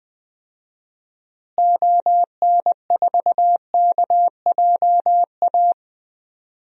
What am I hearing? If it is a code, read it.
Morse code, heard as a keyed tone, about 20 wpm, 700 Hz. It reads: ON4KJA
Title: DXCC-100 Call Signs - 6 Characters - Repeated 2x - 20wpm